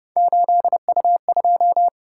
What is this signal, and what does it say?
Morse code, heard as a keyed tone, about 30 wpm, 700 Hz. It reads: 8U2